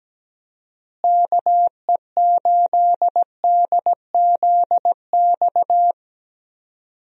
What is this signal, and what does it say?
Morse code, heard as a keyed tone, about 17 wpm, 700 Hz. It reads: KE8DZX
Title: USA Call Signs - 6 Characters - Repeated 3x - 17wpm